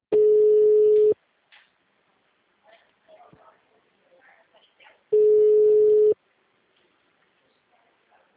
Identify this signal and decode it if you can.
Ringback tone (Europe, 425 Hz), with speech.